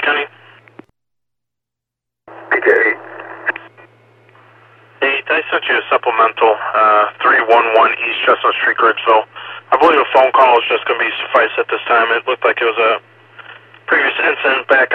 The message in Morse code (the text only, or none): none